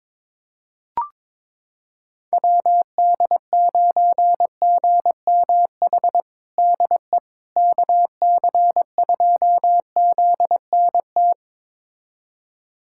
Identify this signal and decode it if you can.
Morse code, heard as a keyed tone, about 21 wpm, 700 Hz. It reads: WD9GMH DE KC2ZNT
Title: USA Callsign DE Callsign 22wpm